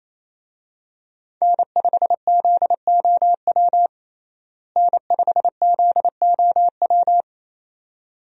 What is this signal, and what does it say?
Morse code, heard as a keyed tone, about 28 wpm, 700 Hz. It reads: N5ZOW N5ZOW
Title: USA Call Signs - 5 Characters - Repeated 2x - 28wpm